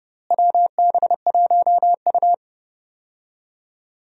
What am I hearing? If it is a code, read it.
Morse code, heard as a keyed tone, about 30 wpm, 700 Hz. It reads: WB1U
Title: USA Call Signs - 4 Characters - Repeated 3x - 30wpm